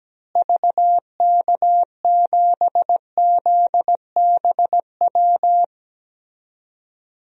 Morse code, heard as a keyed tone, about 17 wpm, 700 Hz. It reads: VK7ZBW
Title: DXCC-100 Call Signs - 6 Characters - Repeated 2x - 17wpm